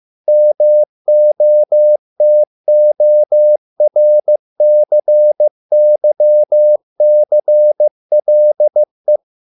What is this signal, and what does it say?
Morse code, heard as a keyed tone, about 15 wpm, 600 Hz. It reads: MOTORCYCLE